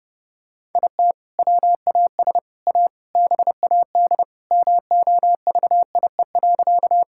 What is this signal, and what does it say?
Morse code, heard as a keyed tone, about 32 wpm, 700 Hz. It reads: IT WAS A BAD MOVIE.
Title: Sentences from Top 1000 Words - Review Part 5 - 30wpm